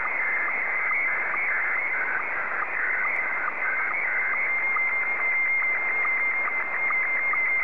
SSTV (slow-scan television), Scottie 1.